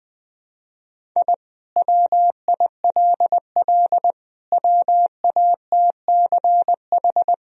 Morse code, heard as a keyed tone, about 21 wpm, 700 Hz. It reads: I WILL WATCH
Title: Sentences from Top 1000 Words - Review Part 1 - 20wpm